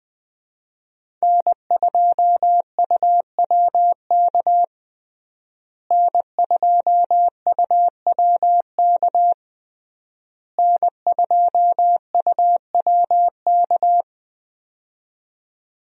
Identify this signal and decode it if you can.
Morse code, heard as a keyed tone, about 20 wpm, 700 Hz. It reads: N2UWK N2UWK N2UWK
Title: USA Call Signs - 5 Characters - Repeated 3x - 20wpm